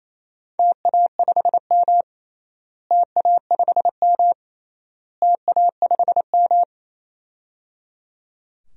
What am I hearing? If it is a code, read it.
Morse code, heard as a keyed tone, about 28 wpm, 700 Hz. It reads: TA5M TA5M TA5M
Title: DXCC-100 Call Signs - 4 Characters - Repeated 3x - 28wpm